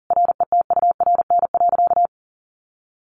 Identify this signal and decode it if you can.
Morse code, heard as a keyed tone, about 40 wpm, 700 Hz. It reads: RETURN.
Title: A Sound Of Thunder 40wpm